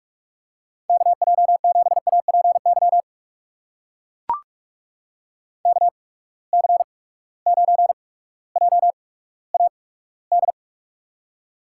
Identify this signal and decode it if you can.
Morse code, heard as a keyed tone, about 44 wpm, spaced as for 19 wpm, 700 Hz. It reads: KJ7APY KC9JAD
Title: USA Call Signs - 6 Characters - 8x Character Spacing - 45wpm